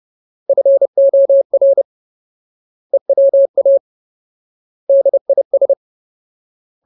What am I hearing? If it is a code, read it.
Morse code, heard as a keyed tone, about 30 wpm, 550 Hz. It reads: FOR EWA DIS
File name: Ultimate Word List / 30WPM Ultimate